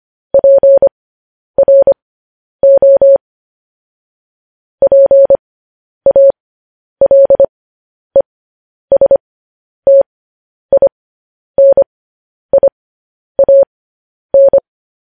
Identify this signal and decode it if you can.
Morse code, heard as a keyed tone, about 25 wpm, spaced as for 10 wpm, 550 Hz. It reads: PRO PALESTINIAN
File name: podcast / News headlines-1x-25x10